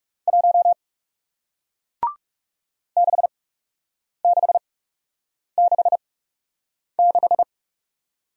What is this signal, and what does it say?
Morse code, heard as a keyed tone, about 41 wpm, spaced as for 8 wpm, 700 Hz. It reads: 1 6666